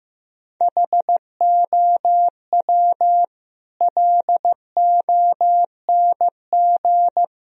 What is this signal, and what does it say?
Morse code, heard as a keyed tone, about 15 wpm, 700 Hz. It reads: HOW LONG
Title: Sentences from Top 1000 Words - Review Part 3 - 15wpm